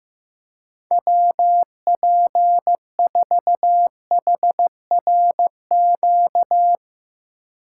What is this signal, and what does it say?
Morse code, heard as a keyed tone, about 15 wpm, 700 Hz. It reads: WP4HRQ
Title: USA Call Signs - 6 Characters - 4x Character Spacing - 15wpm